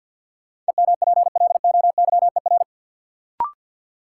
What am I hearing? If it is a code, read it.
Morse code, heard as a keyed tone, about 47 wpm, 700 Hz. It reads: EMPLOYER
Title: Sentences from Top 1600 Words - New Word Drill Alt - 50wpm